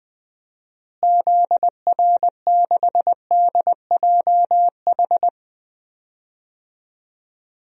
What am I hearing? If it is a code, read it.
Morse code, heard as a keyed tone, about 20 wpm, 700 Hz. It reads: ZR6DJH
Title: DXCC-100 Call Signs - 6 Characters - Repeated 2x - 20wpm